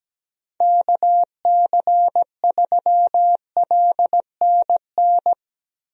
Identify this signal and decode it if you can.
Morse code, heard as a keyed tone, about 17 wpm, 700 Hz. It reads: KC3LNN
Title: USA Call Signs - 6 Characters - 17wpm